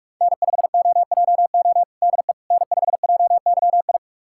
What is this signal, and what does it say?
Morse code, heard as a keyed tone, about 44 wpm, 700 Hz. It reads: N5OJO DE N5JYI
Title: USA Callsign DE Callsign - 5 Characters - 45wpm